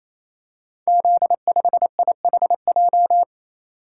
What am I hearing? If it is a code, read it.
Morse code, heard as a keyed tone, about 27 wpm, 700 Hz. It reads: Z5IHJ
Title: Code Groups - 5 Letter:Numbers - 28wpm